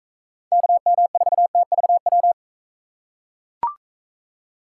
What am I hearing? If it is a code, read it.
Morse code, heard as a keyed tone, about 42 wpm, 700 Hz. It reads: KM4TVW